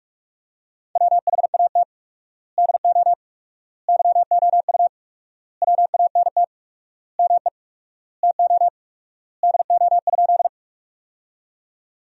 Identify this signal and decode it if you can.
Morse code, heard as a keyed tone, about 41 wpm, 700 Hz. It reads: WHAT DO YOU WANT ME TO DO?
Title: 4x Word-Spacing - Sentences from Top 200 Words - 45wpm